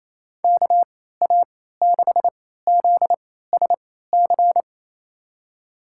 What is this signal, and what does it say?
Morse code, heard as a keyed tone, about 28 wpm, spaced as for 16 wpm, 700 Hz. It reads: KA6ZSC
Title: USA Call Signs - 6 Characters - 3x Character Spacing - 28wpm